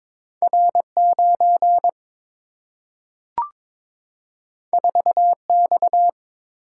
Morse code, heard as a keyed tone, about 22 wpm, 700 Hz. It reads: R 9 4 X
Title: Single-Letter Single-Number 22wpm